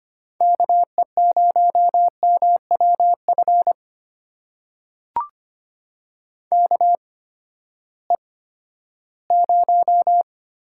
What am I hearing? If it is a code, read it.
Morse code, heard as a keyed tone, about 25 wpm, spaced as for 14 wpm, 700 Hz. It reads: KE0MWF KE0
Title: USA Call Signs - 6 Characters - 8x Character Spacing - 25wpm